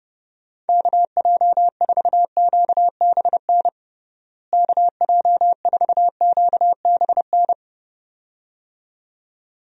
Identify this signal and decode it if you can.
Morse code, heard as a keyed tone, about 30 wpm, 700 Hz. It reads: KJ4QBN KJ4QBN
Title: USA Call Signs - 6 Characters - Repeated 2x - 30wpm